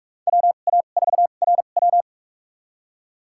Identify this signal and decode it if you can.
Morse code, heard as a keyed tone, about 46 wpm, spaced as for 33 wpm, 690 Hz. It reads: WA4RW
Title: USA Call Signs - 5 Characters - 2x Character Spacing - 45wpm